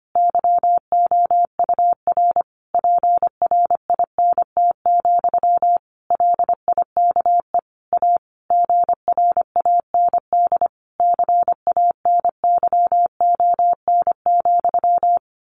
Morse code, heard as a keyed tone, about 25 wpm, 700 Hz. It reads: YOUR PRINT, LIKE A GRAND CANYON,
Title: A Sound Of Thunder 25wpm